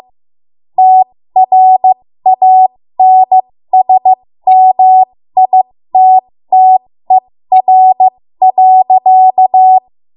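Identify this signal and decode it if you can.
Morse code, heard as a keyed tone, about 15 wpm, spaced as for 13 wpm, 750 Hz. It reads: TRANSMITTER.
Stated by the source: 13 wpm Morse code practice archive transmission.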